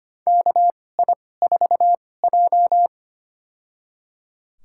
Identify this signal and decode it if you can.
Morse code, heard as a keyed tone, about 25 wpm, spaced as for 18 wpm, 700 Hz. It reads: KI4J
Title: USA Call Signs - 2x Character Spacing - 25wpm